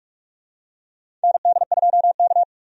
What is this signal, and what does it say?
Morse code, heard as a keyed tone, about 45 wpm, 700 Hz. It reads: ND2K
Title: USA Call Signs - 4 Characters - 45wpm